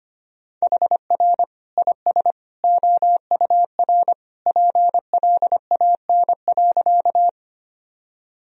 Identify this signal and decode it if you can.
Morse code, heard as a keyed tone, about 25 wpm, 700 Hz. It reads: HR IS OUR PLAN.
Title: Sentences from Top 500 Words and Abbreviations 25wpm